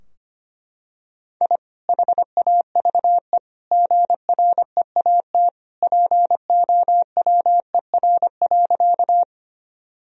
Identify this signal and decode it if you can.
Morse code, heard as a keyed tone, about 25 wpm, 700 Hz. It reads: I HAVE GREAT POWER.
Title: Sentences from Top 1000 Words - Review Part 2 - 25wpm